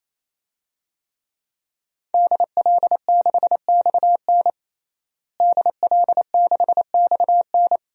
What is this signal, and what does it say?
Morse code, heard as a keyed tone, about 28 wpm, 700 Hz. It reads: DL6XN DL6XN
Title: DXCC-100 Call Signs - 5 Characters - Repeated 3x - 28wpm